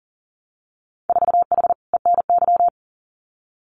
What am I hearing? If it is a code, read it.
Morse code, heard as a keyed tone, about 41 wpm, 700 Hz. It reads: 4H ENY
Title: Winter Field Day Exchanges Encore 40wpm